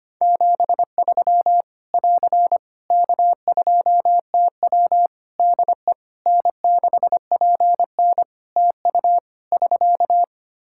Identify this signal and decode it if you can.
Morse code, heard as a keyed tone, about 25 wpm, 700 Hz. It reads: 73 <AR> K2TW DE N6PN TU <SK>
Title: Practice QSO - Exchange 4 - Ending - 25wpm